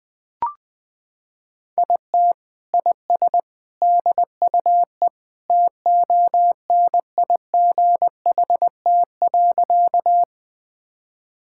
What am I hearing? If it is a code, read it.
Morse code, heard as a keyed tone, about 20 wpm, 700 Hz. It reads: IT IS DUE TONIGHT.
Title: Sentences from Top 1000 Words - Review Part 6 - 20wpm